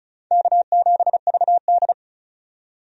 Morse code, heard as a keyed tone, about 36 wpm, 700 Hz. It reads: K7VD
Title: USA Call Signs - 4 Characters - 35wpm